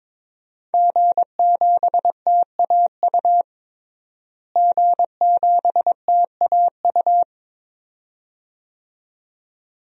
Morse code, heard as a keyed tone, about 21 wpm, 700 Hz. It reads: G7TAU G7TAU
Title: DXCC-100 Call Signs - 5 Characters - Repeated 2x - 22wpm